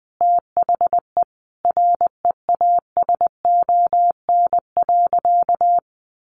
Morse code, heard as a keyed tone, about 20 wpm, 700 Hz. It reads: THE REASON.
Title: Sentences from Top 1400 Words 20wpm